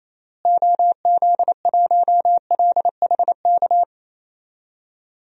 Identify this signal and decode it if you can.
Morse code, heard as a keyed tone, about 28 wpm, 700 Hz. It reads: OZ1LHK
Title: DXCC-100 Call Signs - 6 Characters - Repeated 2x - 28wpm